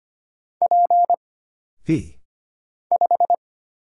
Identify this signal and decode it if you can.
Morse code, heard as a keyed tone, about 25 wpm, 700 Hz. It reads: P 5
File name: Single Character (Letter:Number:Punctuation:Prosign) - Mind-Melt - 25wpm